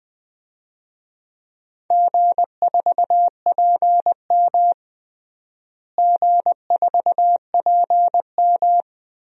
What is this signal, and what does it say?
Morse code, heard as a keyed tone, about 20 wpm, 700 Hz. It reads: G4PM G4PM
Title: DXCC-100 Call Signs - 4 Characters - Repeated 3x - 20wpm